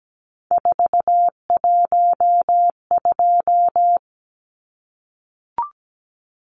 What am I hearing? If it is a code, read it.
Morse code, heard as a keyed tone, about 17 wpm, 700 Hz. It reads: 412
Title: Numbers - 3-Digit 17wpm